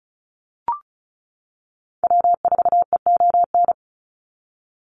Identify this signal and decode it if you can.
Morse code, heard as a keyed tone, about 34 wpm, 700 Hz. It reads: W4EON
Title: NA SOTA Activator Callsigns 35wpm